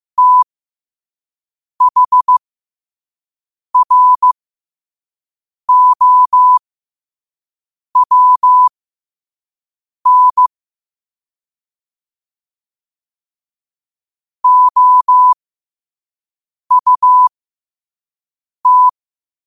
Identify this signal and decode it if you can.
Morse code, heard as a keyed tone, about 15 wpm, spaced as for 5 wpm, 1000 Hz. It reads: THROWN OUT
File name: qotdfiles / 5wpm